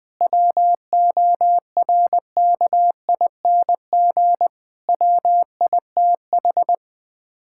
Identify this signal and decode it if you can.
Morse code, heard as a keyed tone, about 20 wpm, 700 Hz. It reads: WORKING WITH
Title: Sets of 6 Words - 20wpm